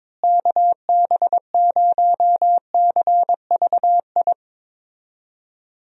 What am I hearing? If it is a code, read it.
Morse code, heard as a keyed tone, about 22 wpm, 700 Hz. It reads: KB0CVI
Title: USA Call Signs - 6 Characters - Repeated 2x - 22wpm